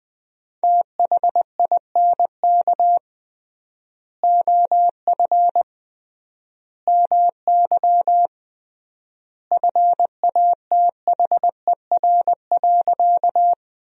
Morse code, heard as a keyed tone, about 20 wpm, 700 Hz. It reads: THINK OF MY FATHER.
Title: 3x Word-Spacing - Sentences from Top 300 Words - 20wpm